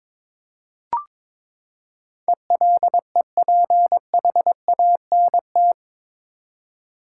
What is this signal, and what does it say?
Morse code, heard as a keyed tone, about 22 wpm, 700 Hz. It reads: ELEPHANT
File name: practice / 22wpm / Eight Letter Words - 22wpm